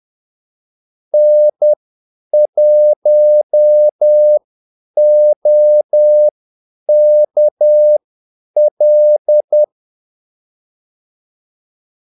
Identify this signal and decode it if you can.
Morse code, heard as a keyed tone, about 10 wpm, spaced as for 8 wpm, 600 Hz. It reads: N1OKL
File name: Callsigns-10-8wpm